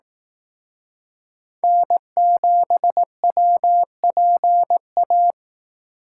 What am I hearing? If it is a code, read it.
Morse code, heard as a keyed tone, about 18 wpm, 700 Hz. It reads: N7WPA
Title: Speed Racing - USA Call Signs 15wpm